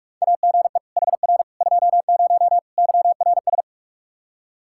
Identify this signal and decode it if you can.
Morse code, heard as a keyed tone, about 46 wpm, 700 Hz. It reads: AGE HR 20 YRS
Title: K1USN SST QSO Practice 45wpm